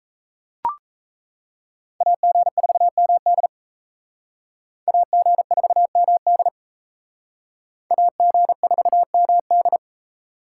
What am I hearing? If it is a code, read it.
Morse code, heard as a keyed tone, about 36 wpm, 700 Hz. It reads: AG4MD AG4MD AG4MD